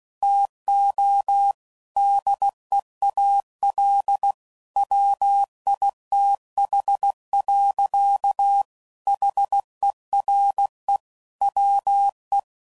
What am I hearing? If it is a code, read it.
Morse code, heard as a keyed tone, about 16 wpm, 800 Hz. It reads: TO DEAL WITH. HERE WE
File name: CW Code Groups @15WPM / Technical Subjects / AND Gate